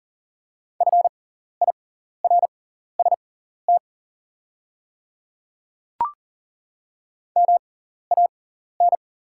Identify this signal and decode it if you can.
Morse code, heard as a keyed tone, about 40 wpm, spaced as for 14 wpm, 690 Hz. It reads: FIRST MAN
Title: 6x Character Spacing - Top 300 Words - 40wpm